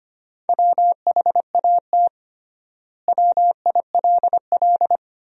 Morse code, heard as a keyed tone, about 25 wpm, 700 Hz. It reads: WHAT WILL
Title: 3x Word-Spacing - Sentences from Top 400 Words - 25wpm